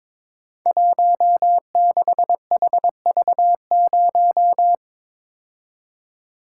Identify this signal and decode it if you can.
Morse code, heard as a keyed tone, about 22 wpm, 700 Hz. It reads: 16HV0
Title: Code Groups - 5 Letter:Numbers - 22wpm